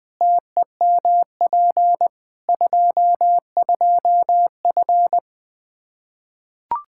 Morse code, heard as a keyed tone, about 20 wpm, 700 Hz. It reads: TEMP 22F
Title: Practice QSO - Exchange 2 - Info - 20wpm